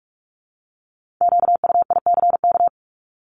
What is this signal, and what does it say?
Morse code, heard as a keyed tone, about 46 wpm, 700 Hz. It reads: QUICK
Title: Multi-letter Phonemes 45wpm